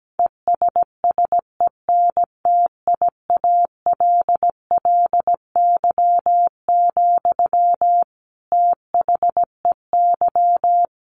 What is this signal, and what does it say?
Morse code, heard as a keyed tone, about 17 wpm, 700 Hz. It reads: ESSENTIALLY, THEY
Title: Sentences from Top 1300 Words 17wpm